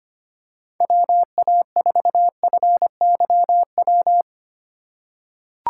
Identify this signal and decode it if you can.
Morse code, heard as a keyed tone, about 25 wpm, 700 Hz. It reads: WA4FYW
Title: USA Call Signs - 6 Characters - 3x Character Spacing - 25wpm